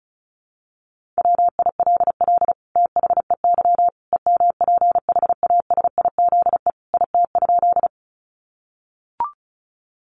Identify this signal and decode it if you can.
Morse code, heard as a keyed tone, about 34 wpm, 700 Hz. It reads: WILL THEY EMPHASIZE IT?
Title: Sentences from Top 1500 Words 35wpm